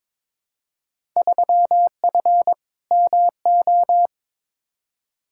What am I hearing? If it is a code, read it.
Morse code, heard as a keyed tone, about 21 wpm, 700 Hz. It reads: 3F MO
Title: Field Day Exchanges Encore - 22wpm